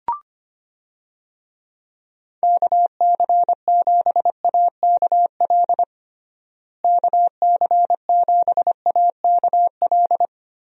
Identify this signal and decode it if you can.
Morse code, heard as a keyed tone, about 25 wpm, 700 Hz. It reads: KC7AKL KC7AKL
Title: USA Call Signs - 6 Characters - Repeated 2x - 25wpm